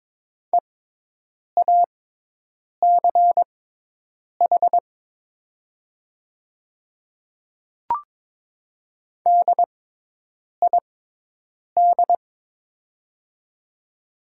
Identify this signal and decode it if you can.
Morse code, heard as a keyed tone, about 22 wpm, spaced as for 8 wpm, 700 Hz. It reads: EACH DID
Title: 6x Character Spacing - Top 100 Words - 22wpm